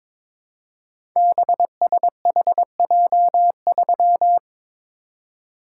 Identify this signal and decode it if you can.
Morse code, heard as a keyed tone, about 22 wpm, 700 Hz. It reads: BSHJ3
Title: Code Groups - 5 Letter:Numbers - 22wpm